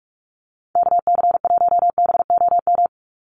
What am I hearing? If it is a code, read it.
Morse code, heard as a keyed tone, about 44 wpm, 700 Hz. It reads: KC1BOM
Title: USA Call Signs 45wpm